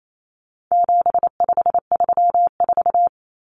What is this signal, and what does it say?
Morse code, heard as a keyed tone, about 28 wpm, 700 Hz. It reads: 7534
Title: Common Cut Numbers - 4-Digit 28wpm